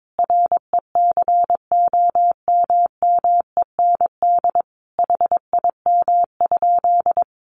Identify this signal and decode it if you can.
Morse code, heard as a keyed tone, about 22 wpm, 700 Hz. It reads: RECOMMEND HIM?
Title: Sentences from Top 1200 Words 22wpm